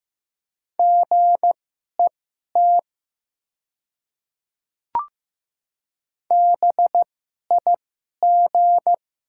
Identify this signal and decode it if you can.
Morse code, heard as a keyed tone, about 15 wpm, spaced as for 11 wpm, 700 Hz. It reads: GET BIG
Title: 2x Character Spacing - Top 200 Words - 15wpm